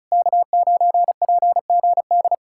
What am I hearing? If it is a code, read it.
Morse code, heard as a keyed tone, about 40 wpm, 700 Hz. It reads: K9PGD